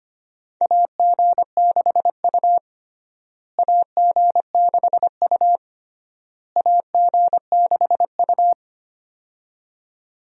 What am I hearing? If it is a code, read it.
Morse code, heard as a keyed tone, about 25 wpm, 700 Hz. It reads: AG6U AG6U AG6U
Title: USA Call Signs - 4 Characters - Repeated 3x - 25wpm